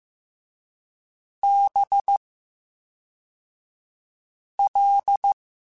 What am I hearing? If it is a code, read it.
Morse code, heard as a keyed tone, about 15 wpm, 800 Hz. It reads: B L
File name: Special INGs / Special INGs-mp3s / VerySlowToFaster Ing @10